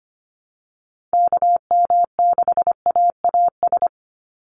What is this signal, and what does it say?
Morse code, heard as a keyed tone, about 25 wpm, 700 Hz. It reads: KM6AAS
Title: USA Call Signs 25wpm